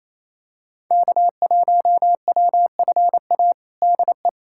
Morse code, heard as a keyed tone, about 28 wpm, 700 Hz. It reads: K1WFA DE
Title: USA Callsign DE Callsign - 5 Characters - 28wpm